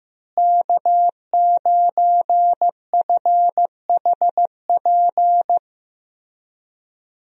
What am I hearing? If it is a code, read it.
Morse code, heard as a keyed tone, about 15 wpm, 700 Hz. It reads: K9FHP
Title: USA Call Signs - 5 Characters - Repeated 3x - 15wpm